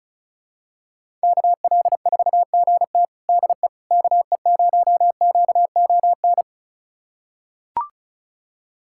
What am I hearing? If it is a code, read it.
Morse code, heard as a keyed tone, about 36 wpm, 700 Hz. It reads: KL4GT DE KE0QON
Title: USA Callsign DE Callsign 35wpm